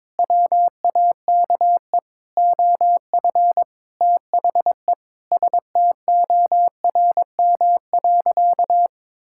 Morse code, heard as a keyed tone, about 22 wpm, 700 Hz. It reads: WAKE OF THE STORM.
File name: practice / 22wpm / Sentences with 4-Word Phrases - 22wpm